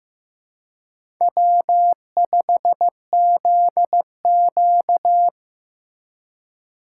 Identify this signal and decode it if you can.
Morse code, heard as a keyed tone, about 15 wpm, 700 Hz. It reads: W5ZQ
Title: USA Call Signs - 4 Characters - 4x Character Spacing - 15wpm